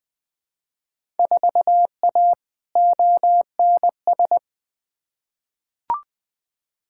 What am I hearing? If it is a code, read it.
Morse code, heard as a keyed tone, about 21 wpm, 700 Hz. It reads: 4A ONS
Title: Field Day Exchanges - 20wpm